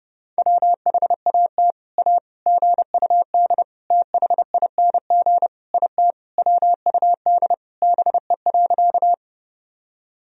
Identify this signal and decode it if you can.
Morse code, heard as a keyed tone, about 30 wpm, 700 Hz. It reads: WHAT A GUD THING IT WUD BE.
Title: Sentences from Top 200 Words and Abbreviations 30wpm